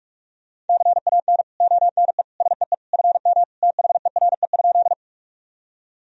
Morse code, heard as a keyed tone, about 46 wpm, 700 Hz. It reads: CAN ONE SEE FM THERE?
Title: Sentences from Top 100 Words and Abbreviations 45wpm